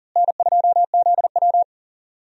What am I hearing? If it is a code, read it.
Morse code, heard as a keyed tone, about 40 wpm, 700 Hz. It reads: N2ZW
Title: USA Call Signs - 4 Characters - Repeated 2x - 40wpm